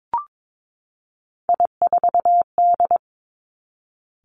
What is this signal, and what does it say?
Morse code, heard as a keyed tone, about 22 wpm, 700 Hz. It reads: I4D
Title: Three Challenging Letter-Number Combinations 22wpm